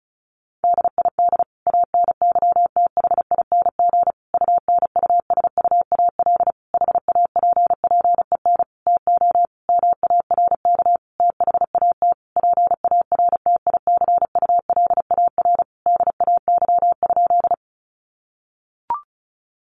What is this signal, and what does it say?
Morse code, heard as a keyed tone, about 34 wpm, 700 Hz. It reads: DID ANYTHING UNUSUAL HAPPEN TO MARK THAT PARTICULAR DAY?
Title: Sentences from Top 1400 Words 35wpm